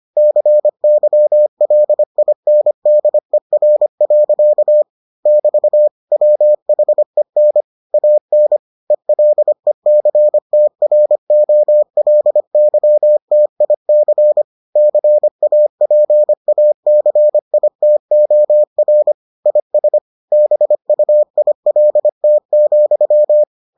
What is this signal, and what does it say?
Morse code, heard as a keyed tone, about 25 wpm, 600 Hz. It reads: CYLINDER. <BT> WHEN AN ELECTROLYTIC CAPACITOR IS BUILT,